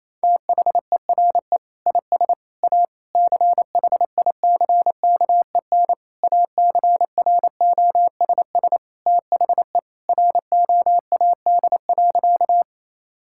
Morse code, heard as a keyed tone, about 28 wpm, 700 Hz. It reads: THERE IS A CHICKEN ACROSS THE ROAD.
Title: Sentences from Top 1000 Words - Review Part 5 - 28wpm